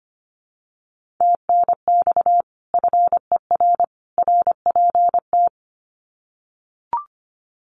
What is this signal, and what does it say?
Morse code, heard as a keyed tone, about 25 wpm, 700 Hz. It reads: TNX FER RPT
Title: Common QSO Phrases 25wpm